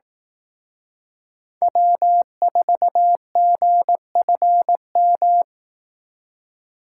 Morse code, heard as a keyed tone, about 18 wpm, 700 Hz. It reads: W4GFM
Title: Speed Racing - USA 1x3 Call Signs 15wpm